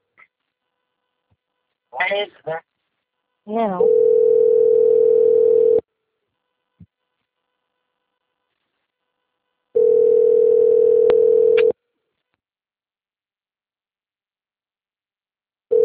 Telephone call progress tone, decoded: ringback tone

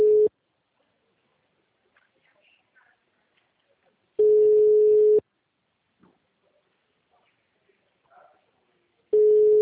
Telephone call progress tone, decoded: ringback tone

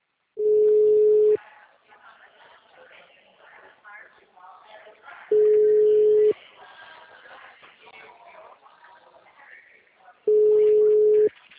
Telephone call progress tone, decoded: ringback tone

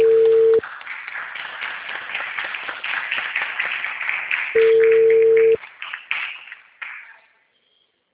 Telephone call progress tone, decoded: ringback tone

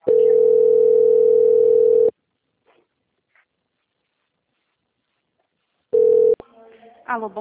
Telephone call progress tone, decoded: ringback tone

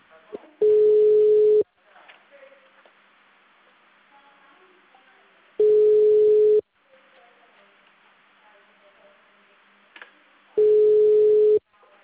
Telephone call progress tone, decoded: ringback tone

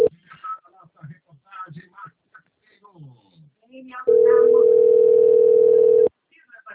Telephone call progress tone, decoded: ringback tone

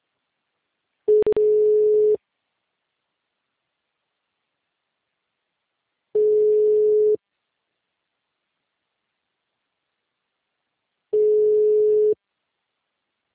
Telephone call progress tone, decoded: ringback tone

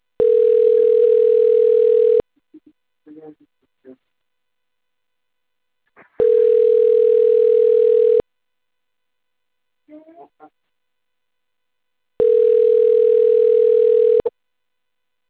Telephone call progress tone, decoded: ringback tone

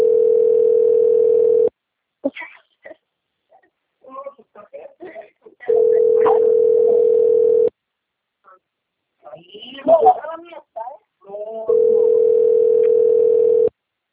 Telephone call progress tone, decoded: ringback tone

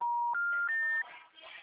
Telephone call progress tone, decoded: special information tone

